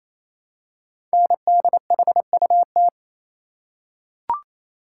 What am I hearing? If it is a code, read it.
Morse code, heard as a keyed tone, about 27 wpm, 700 Hz. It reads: NDHUT
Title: Code Groups - 5 Letters - 28wpm